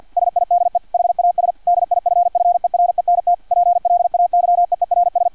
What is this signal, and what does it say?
Morse code, heard as a keyed tone, about 50 wpm, 700 Hz, with radio noise. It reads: SIDE HAS DIFFERENT PLAYEERS
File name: archived / CRSnet